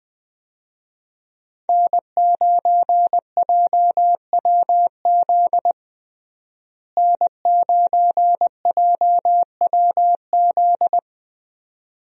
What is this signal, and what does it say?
Morse code, heard as a keyed tone, about 20 wpm, 700 Hz. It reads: N9JWZ N9JWZ
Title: USA Call Signs - 5 Characters - Repeated 2x - 20wpm